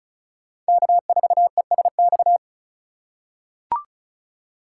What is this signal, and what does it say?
Morse code, heard as a keyed tone, about 36 wpm, 700 Hz. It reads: K4ESX
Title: USA Call Signs - 5 Characters - 2x Character Spacing - 35wpm